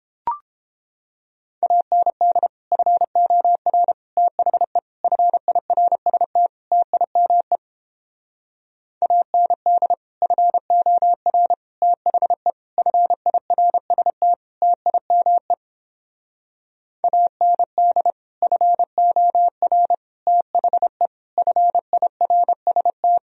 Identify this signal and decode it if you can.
Morse code, heard as a keyed tone, about 29 wpm, 700 Hz. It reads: AND FOR THE FIRST TIME AND FOR THE FIRST TIME AND FOR THE FIRST